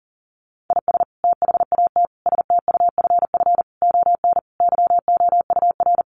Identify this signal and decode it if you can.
Morse code, heard as a keyed tone, about 40 wpm, 700 Hz. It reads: IS THAT STUFF ON YOUR
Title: Sentences from Top 1400 Words 40wpm